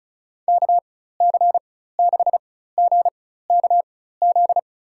Morse code, heard as a keyed tone, about 36 wpm, spaced as for 16 wpm, 700 Hz. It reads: KC6GKZ
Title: USA Call Signs - 4x Character Spacing - 35wpm